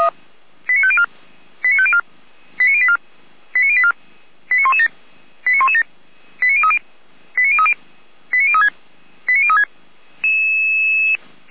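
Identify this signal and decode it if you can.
Five-tone selective call (ZVEI), heard as quick runs of five tones, tones of about 70 ms. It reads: 89504 89504 89084 89084 89117 89117 89030 89030 89036 89036
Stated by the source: ZVEI five-tone call 89080 received on an SDR.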